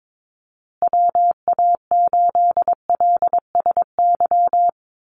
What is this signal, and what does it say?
Morse code, heard as a keyed tone, about 22 wpm, 700 Hz. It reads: WA8LSY
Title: USA Call Signs Repeated 3x 22wpm